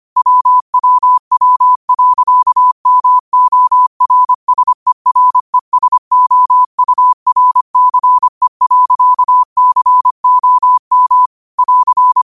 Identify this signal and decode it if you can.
Morse code, heard as a keyed tone, about 25 wpm, 1000 Hz. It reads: WWW.MORSERESOURCE.COM <AR>
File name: qotdfiles / 25wpm